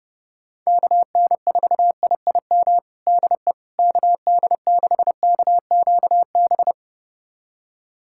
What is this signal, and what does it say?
Morse code, heard as a keyed tone, about 32 wpm, 700 Hz. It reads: KN4IIM DE KD6KQB
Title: USA Callsign DE Callsign 30wpm